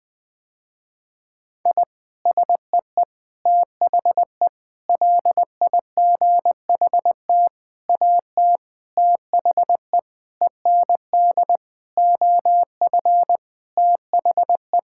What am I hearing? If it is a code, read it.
Morse code, heard as a keyed tone, about 20 wpm, 700 Hz. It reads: I SEE THE LIGHT AT THE END OF THE
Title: Sentences with 8-Word Phrases - Spoken Beginning - 20wpm